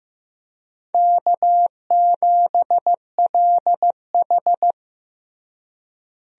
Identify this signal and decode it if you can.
Morse code, heard as a keyed tone, about 16 wpm, 700 Hz. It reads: K7LH
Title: USA Call Signs - 4 Characters - Repeated 2x - 15wpm